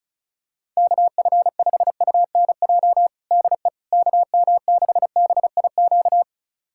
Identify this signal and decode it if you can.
Morse code, heard as a keyed tone, about 36 wpm, 700 Hz. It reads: KF5UNJ DE KM6BIQ
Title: USA Callsign DE Callsign 35wpm